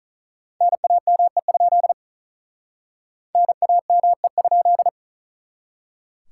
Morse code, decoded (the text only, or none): NAME? NAME?